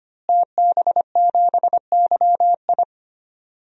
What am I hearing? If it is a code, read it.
Morse code, heard as a keyed tone, about 25 wpm, 700 Hz. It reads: TB7YI
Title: DXCC-100 Call Signs - 5 Characters - Repeated 2x - 25wpm